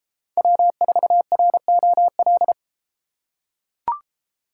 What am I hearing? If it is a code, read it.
Morse code, heard as a keyed tone, about 32 wpm, 700 Hz. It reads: W4ROL